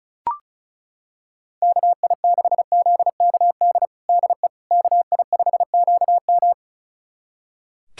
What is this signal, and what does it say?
Morse code, heard as a keyed tone, about 36 wpm, 700 Hz. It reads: KI6ZKD DE KI5QM
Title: USA Callsign DE Callsign 35wpm